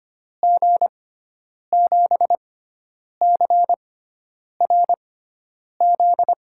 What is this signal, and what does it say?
Morse code, heard as a keyed tone, about 25 wpm, spaced as for 9 wpm, 700 Hz. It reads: G7CRZ
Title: USA Call Signs - 6 Characters - 6x Character Spacing - 25wpm